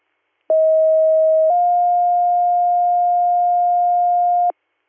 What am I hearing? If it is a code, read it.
Two-tone page, heard as a one-second tone, then a three-second tone, about 640 then 710 Hz. It is heard from 0.5 to 4.5 s.